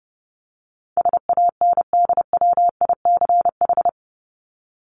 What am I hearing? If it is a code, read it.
Morse code, heard as a keyed tone, about 30 wpm, 700 Hz. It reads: SANDWICH
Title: Top 1751-2000 Words 30wpm